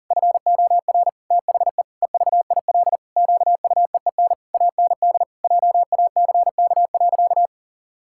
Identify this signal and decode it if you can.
Morse code, heard as a keyed tone, about 40 wpm, 700 Hz. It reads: FOR THE EVIL QUEEN AND JACK.
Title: Pangram Challenge v2 - 40wpm